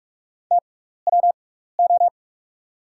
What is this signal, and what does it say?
Morse code, heard as a keyed tone, about 44 wpm, spaced as for 15 wpm, 690 Hz. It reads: TWO
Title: 6x Character Spacing - Top 200 Words - 45wpm